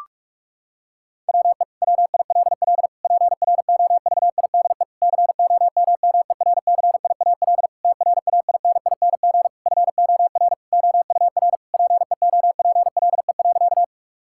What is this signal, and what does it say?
Morse code, heard as a keyed tone, about 41 wpm, 700 Hz. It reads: WE WILL PROVIDE COMMERCIAL TRAINING FOR OUR PEOPLE.
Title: Sentences from Top 1000 Words - Review Part 5 - 45wpm